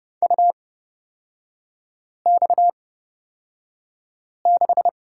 Morse code, heard as a keyed tone, about 30 wpm, spaced as for 5 wpm, 700 Hz. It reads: UX6
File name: Lesson 37e - Letter x - Three Characters 30wpm